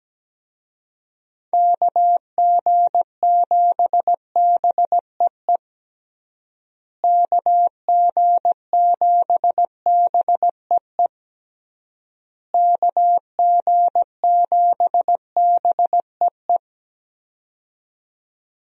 Morse code, heard as a keyed tone, about 17 wpm, 700 Hz. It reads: KG7BEE KG7BEE KG7BEE
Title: USA Call Signs - 6 Characters - Repeated 3x - 17wpm